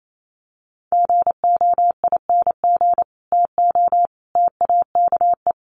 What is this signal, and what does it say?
Morse code, heard as a keyed tone, about 28 wpm, 700 Hz. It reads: GOING TO TAKE
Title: Sets of 3 Words 28wpm